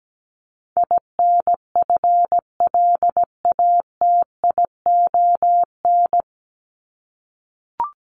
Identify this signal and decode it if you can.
Morse code, heard as a keyed tone, about 17 wpm, 700 Hz. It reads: INFLATION
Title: Top 1001-1250 Words 17wpm